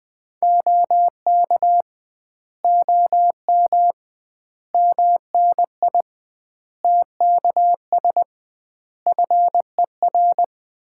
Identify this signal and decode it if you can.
Morse code, heard as a keyed tone, about 21 wpm, 700 Hz. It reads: OK OM MNI TKS FER
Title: Common QSO Phrases - 2x Word-Spacing - 20wpm